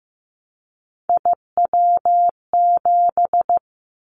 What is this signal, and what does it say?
Morse code, heard as a keyed tone, about 15 wpm, 700 Hz. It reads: IW7
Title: Callsign Prefixes for Top 25 DXCC - 15wpm